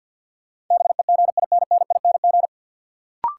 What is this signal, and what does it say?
Morse code, heard as a keyed tone, about 50 wpm, 700 Hz. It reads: BEGINNING
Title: Sets of 3 Words - New Word Drill - 50wpm